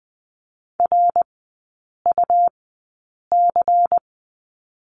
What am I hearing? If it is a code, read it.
Morse code, heard as a keyed tone, about 20 wpm, spaced as for 8 wpm, 700 Hz. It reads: RUC
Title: 114 Lesson 19e - Letter w - Three Characters 20wpm